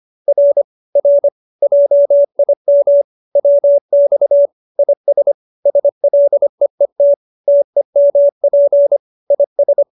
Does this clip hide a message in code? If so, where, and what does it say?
Yes: Morse code from 0.3 to 9.8 s, heard as a keyed tone, about 25 wpm, 570 Hz. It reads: R R JIM WX IS SLEET TEMP IS